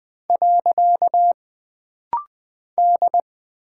Morse code, heard as a keyed tone, about 20 wpm, 700 Hz. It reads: . D
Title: Single Character (Letter:Number:Punctuation:Prosign) - Rapid-Fire - 20wpm